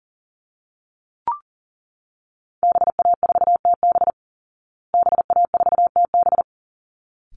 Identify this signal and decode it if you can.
Morse code, heard as a keyed tone, about 40 wpm, 700 Hz. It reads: BA4TB BA4TB
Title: Call Signs Repeated 3x - 40wpm